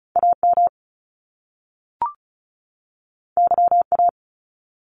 Morse code, heard as a keyed tone, about 36 wpm, 700 Hz. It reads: AM YA
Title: Top Two Letter Combinations in Words 35wpm